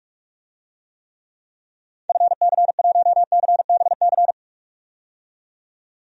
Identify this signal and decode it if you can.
Morse code, heard as a keyed tone, about 45 wpm, 700 Hz. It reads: FC1CBC
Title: DXCC-100 Call Signs - 6 Characters - 45wpm